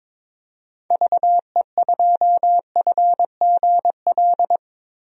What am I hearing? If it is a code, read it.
Morse code, heard as a keyed tone, about 22 wpm, 700 Hz. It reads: VE2FGL
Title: DXCC-100 Call Signs - 6 Characters - Repeated 3x - 22wpm